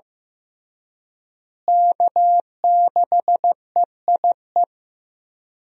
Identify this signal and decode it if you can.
Morse code, heard as a keyed tone, about 16 wpm, 700 Hz. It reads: K6EIE
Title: USA Call Signs - 5 Characters - Repeated 3x - 15wpm